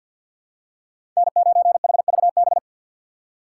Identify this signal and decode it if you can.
Morse code, heard as a keyed tone, about 53 wpm, 700 Hz. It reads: N9HVB
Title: USA Call Signs - 5 Characters - 8x Character Spacing - 50wpm